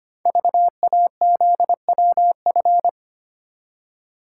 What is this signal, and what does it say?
Morse code, heard as a keyed tone, about 25 wpm, 700 Hz. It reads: VAZWF
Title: Code Groups - 5 Letter:Numbers - 25wpm